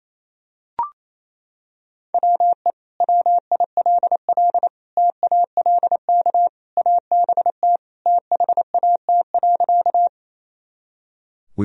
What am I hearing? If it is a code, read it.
Morse code, heard as a keyed tone, about 28 wpm, 700 Hz. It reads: WE WILL TALK ABT THAT.
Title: Sentences from Top 400 Words and Abbreviations 28wpm